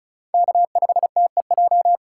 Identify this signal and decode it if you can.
Morse code, heard as a keyed tone, about 43 wpm, 700 Hz. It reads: K5TEJ
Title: Speed Racing - USA 1x3 Call Signs 35wpm